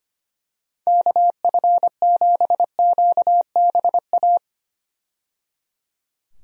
Morse code, heard as a keyed tone, about 25 wpm, 700 Hz. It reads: KF7QBA
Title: USA Call Signs - 6 Characters - 25wpm